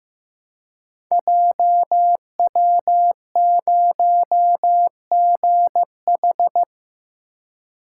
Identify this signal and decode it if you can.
Morse code, heard as a keyed tone, about 15 wpm, 700 Hz. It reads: JW0GH
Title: Code Groups - 5 Letter:Numbers - 15wpm